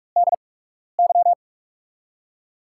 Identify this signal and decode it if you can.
Morse code, heard as a keyed tone, about 45 wpm, spaced as for 12 wpm, 700 Hz. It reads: DY
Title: USA Call Signs - 5 Characters - 8x Character Spacing - 45wpm